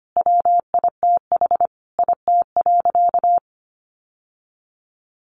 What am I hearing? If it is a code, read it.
Morse code, heard as a keyed tone, about 25 wpm, 700 Hz. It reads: WITH IT.
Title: Sentences from Top 800 Words 25wpm